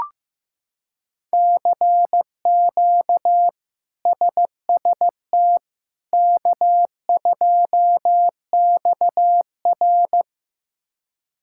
Morse code, heard as a keyed tone, about 15 wpm, 700 Hz. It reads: CQ SST K2XR
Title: K1USN SST QSO Practice - 15wpm